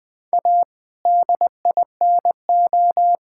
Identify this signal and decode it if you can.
Morse code, heard as a keyed tone, about 20 wpm, 700 Hz. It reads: A DINO
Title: All-Age Appropriate Jokes - 20wpm